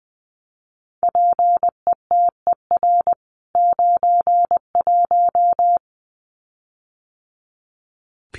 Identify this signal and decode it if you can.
Morse code, heard as a keyed tone, about 20 wpm, 700 Hz. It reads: PETER 91
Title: CWT Exchange 20wpm